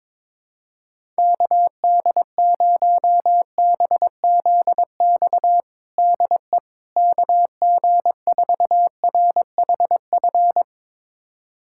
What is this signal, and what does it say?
Morse code, heard as a keyed tone, about 22 wpm, 700 Hz. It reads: KD0BZX DE KG4RHF
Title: USA Callsign DE Callsign - 6 Characters - 22wpm